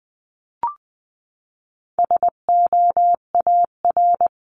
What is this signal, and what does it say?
Morse code, heard as a keyed tone, about 20 wpm, 700 Hz. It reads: SOAR
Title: 259 Lesson 42d - Words 18-20wpm